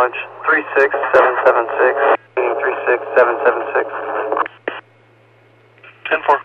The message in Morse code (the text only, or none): none